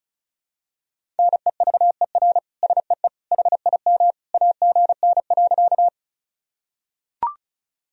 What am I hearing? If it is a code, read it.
Morse code, heard as a keyed tone, about 36 wpm, 700 Hz. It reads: NEVER SEE HIM AGN.
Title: Sentences from Top 300 Words and Abbreviations 35wpm